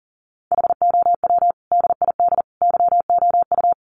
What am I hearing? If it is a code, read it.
Morse code, heard as a keyed tone, about 40 wpm, 700 Hz. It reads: HOW DID YOU
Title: Sentences from Top 1100 Words 40wpm